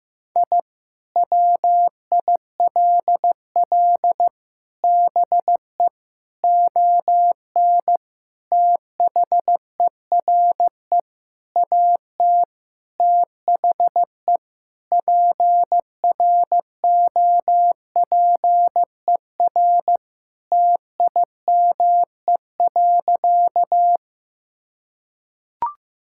Morse code, heard as a keyed tone, about 15 wpm, 700 Hz. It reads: I WILL BE ON THERE AT THE PROPER TIME.
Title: Sentences from Top 1000 Words - Review Part 5 - 15wpm